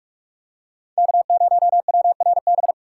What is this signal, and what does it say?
Morse code, heard as a keyed tone, about 42 wpm, 700 Hz. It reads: K0WRB
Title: USA Call Signs - 5 Characters - 2x Character Spacing - 45wpm